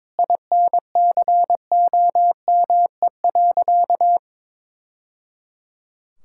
Morse code, heard as a keyed tone, about 21 wpm, 700 Hz. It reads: INCOME.
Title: Sentences from Top 1000 Words - Review Part 5 - 22wpm